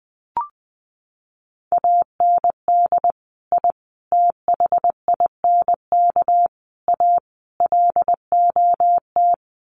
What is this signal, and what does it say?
Morse code, heard as a keyed tone, about 20 wpm, 700 Hz. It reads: AND I THINK A LOT
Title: Sets of 5 Words Encore2 20wpm